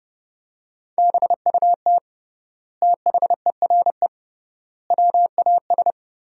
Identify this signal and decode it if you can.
Morse code, heard as a keyed tone, about 30 wpm, 700 Hz. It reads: BUT THERE WAS
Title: 3x Word-Spacing - Sentences from Top 100 Words - 30wpm